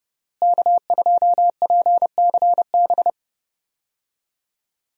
Morse code, heard as a keyed tone, about 30 wpm, 700 Hz. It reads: K2PCB
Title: USA Call Signs - 5 Characters - Repeated 2x - 30wpm